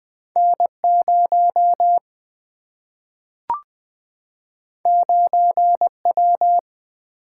Morse code, heard as a keyed tone, about 20 wpm, 700 Hz. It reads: N0 9W
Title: Single-Letter Single-Number 20wpm